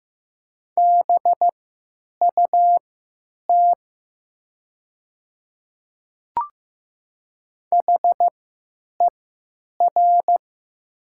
Morse code, heard as a keyed tone, about 15 wpm, spaced as for 9 wpm, 700 Hz. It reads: BUT HER
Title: 3x Character Spacing - Top 100 Words - 15wpm